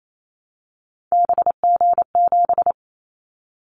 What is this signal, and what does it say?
Morse code, heard as a keyed tone, about 28 wpm, 700 Hz. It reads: BG7
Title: Callsign Prefixes for Top 100 DXCC - 28wpm